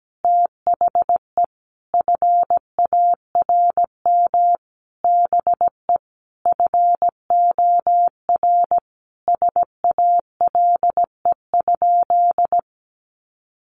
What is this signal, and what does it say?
Morse code, heard as a keyed tone, about 17 wpm, 700 Hz. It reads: THE FARM BE FOR SALE?
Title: Sentences from Top 1400 Words 17wpm